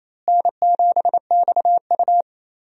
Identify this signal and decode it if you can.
Morse code, heard as a keyed tone, about 28 wpm, 700 Hz. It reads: N7XU
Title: USA Call Signs - 4 Characters - Repeated 3x - 28wpm